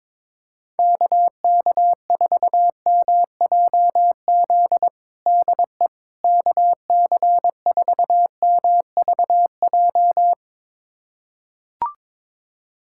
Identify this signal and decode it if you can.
Morse code, heard as a keyed tone, about 22 wpm, 700 Hz. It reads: KK4MJZ DE KC4MVJ
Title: USA Callsign DE Callsign - 6 Characters - 22wpm